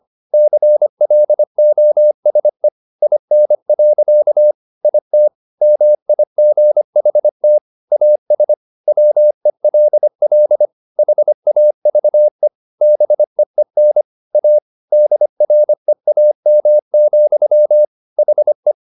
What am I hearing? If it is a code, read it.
Morse code, heard as a keyed tone, about 25 wpm, 600 Hz. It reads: CLOSE IN. IT MIGHT AS WELL HAVE BEEN A DREAM, HE